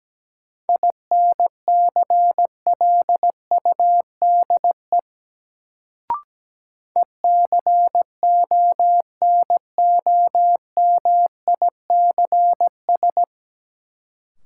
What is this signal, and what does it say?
Morse code, heard as a keyed tone, about 17 wpm, 700 Hz. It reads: INCLUDE ECONOMICS
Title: Sentences from Top 1000 Words - Review Part 3 - New Word Drill - 17wpm